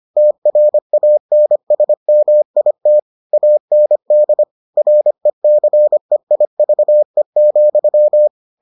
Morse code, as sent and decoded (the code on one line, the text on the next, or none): - .-. .- -. ... -- .. - / .- -. -.. / .-. -.-. .. ...- . --..--
TRANSMIT AND RECEIVE,